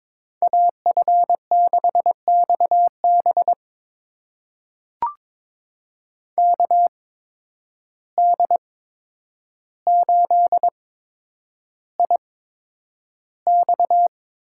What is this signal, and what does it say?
Morse code, heard as a keyed tone, about 21 wpm, spaced as for 9 wpm, 700 Hz. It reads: AF6XB KD8IX
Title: USA Call Signs - 5 Characters - 8x Character Spacing - 22wpm